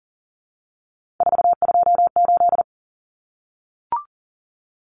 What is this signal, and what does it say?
Morse code, heard as a keyed tone, about 40 wpm, 700 Hz. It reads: 428
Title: Common Cut Numbers - 3-Digit 40wpm